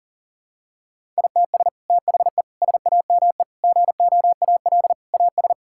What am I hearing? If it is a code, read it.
Morse code, heard as a keyed tone, about 40 wpm, 700 Hz. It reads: ITS THE SAME GOAL AS
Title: Sentences from Top 1000 Words - Review Part 5 - 40wpm